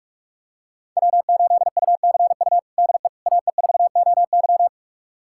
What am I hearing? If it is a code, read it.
Morse code, heard as a keyed tone, about 45 wpm, 700 Hz. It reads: W8UCU DE AE4OY
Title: USA Callsign DE Callsign 45wpm